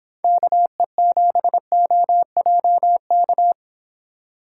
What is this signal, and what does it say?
Morse code, heard as a keyed tone, about 24 wpm, 700 Hz. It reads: KE7OJK